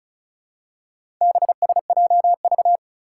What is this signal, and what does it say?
Morse code, heard as a keyed tone, about 36 wpm, 700 Hz. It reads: BSJV
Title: Four Challenging Letter-Number Combinations 35wpm